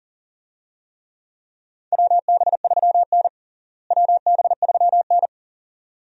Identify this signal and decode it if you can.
Morse code, heard as a keyed tone, about 40 wpm, 700 Hz. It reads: WB3N WB3N
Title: USA Call Signs - 4 Characters - Repeated 2x - 40wpm